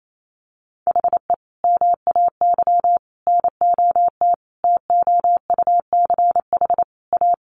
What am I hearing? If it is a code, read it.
Morse code, heard as a keyed tone, about 28 wpm, 700 Hz. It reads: HE MAY NOT TOUCH A
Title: Sentences from Top 1500 Words 28wpm